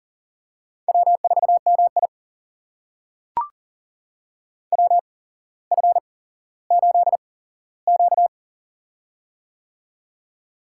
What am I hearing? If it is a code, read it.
Morse code, heard as a keyed tone, about 37 wpm, spaced as for 17 wpm, 700 Hz. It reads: W4MI WF8Q
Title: USA Call Signs - 8x Character Spacing - 40wpm